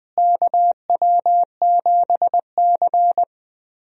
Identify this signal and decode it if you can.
Morse code, heard as a keyed tone, about 20 wpm, 700 Hz. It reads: KW7C
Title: USA Call Signs - 4 Characters - Repeated 2x - 20wpm